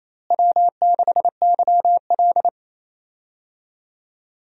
Morse code, heard as a keyed tone, about 28 wpm, 700 Hz. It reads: W6YL
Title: USA Call Signs - 4 Characters - Repeated 3x - 28wpm